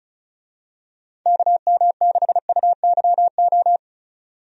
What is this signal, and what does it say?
Morse code, heard as a keyed tone, about 34 wpm, 700 Hz. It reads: KM6UYO
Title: USA Call Signs - 6 Characters - 35wpm